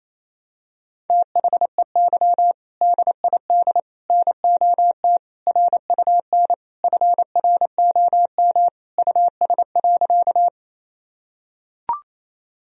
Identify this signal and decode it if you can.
Morse code, heard as a keyed tone, about 28 wpm, 700 Hz. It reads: THEY DID NOT RUN FROM US.
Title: Sentences from Top 1000 Words - Review Part 1 - 28wpm